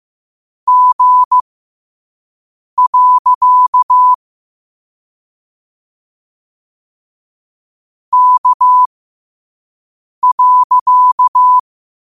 Morse code, heard as a keyed tone, about 15 wpm, spaced as for 5 wpm, 1000 Hz. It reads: G. K.
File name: qotdfiles / 5wpm